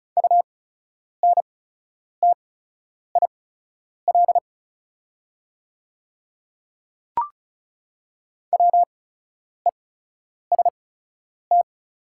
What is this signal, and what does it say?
Morse code, heard as a keyed tone, about 36 wpm, spaced as for 10 wpm, 690 Hz. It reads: UNTIL WEST
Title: 8x Character Spacing - Top 500 Words - 35wpm